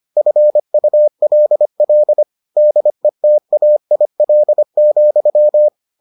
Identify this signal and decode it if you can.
Morse code, heard as a keyed tone, about 25 wpm, 600 Hz. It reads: FULL DETAIL,